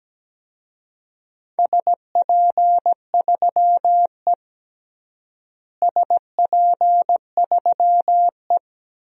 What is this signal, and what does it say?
Morse code, heard as a keyed tone, about 17 wpm, 700 Hz. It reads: SP3E SP3E
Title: DXCC-100 Call Signs - 4 Characters - Repeated 2x - 17wpm